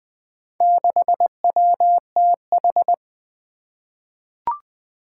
Morse code, heard as a keyed tone, about 20 wpm, 700 Hz. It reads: 6WTH
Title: Four Challenging Letter-Number Combinations 20wpm